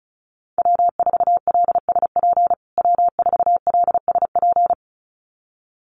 Morse code, heard as a keyed tone, about 34 wpm, 700 Hz. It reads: W4LSP W4LSP
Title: USA Call Signs Repeated 2x 35wpm